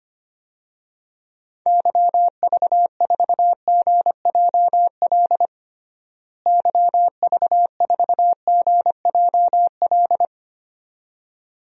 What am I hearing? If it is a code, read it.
Morse code, heard as a keyed tone, about 25 wpm, 700 Hz. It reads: YV4GJL YV4GJL
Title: DXCC-100 Call Signs - 6 Characters - Repeated 2x - 25wpm